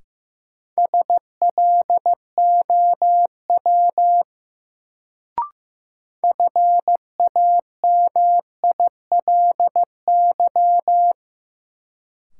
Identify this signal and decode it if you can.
Morse code, heard as a keyed tone, about 15 wpm, 700 Hz. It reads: SLOW FAMILY
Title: Sentences from Top 400 Words - New Word Drill - 15wpm